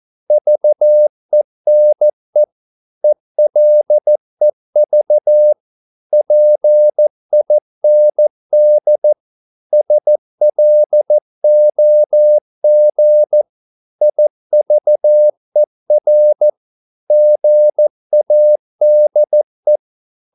Morse code, heard as a keyed tone, about 14 wpm, 600 Hz. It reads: VENE ELEV PIND SLOG IVER GADE
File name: Korte ord DK 14wpm